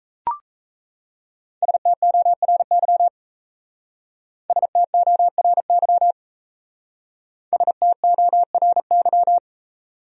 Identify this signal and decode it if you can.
Morse code, heard as a keyed tone, about 34 wpm, 700 Hz. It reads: STORY STORY STORY
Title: Speed-Racing Top 300 Words 28wpm